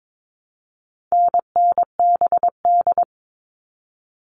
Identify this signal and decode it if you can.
Morse code, heard as a keyed tone, about 22 wpm, 700 Hz. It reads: NNBD
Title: All Cut Numbers - 4-Digit 22wpm